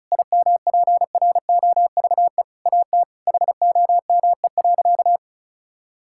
Morse code, heard as a keyed tone, about 36 wpm, 700 Hz. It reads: IMPROVE AT HOME.
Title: Sentences from Top 1000 Words - Review Part 5 - 35wpm